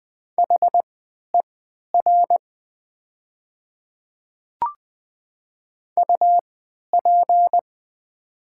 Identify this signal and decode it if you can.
Morse code, heard as a keyed tone, about 20 wpm, spaced as for 11 wpm, 700 Hz. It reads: HER UP
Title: 3x Character Spacing - Top 300 Words - 20wpm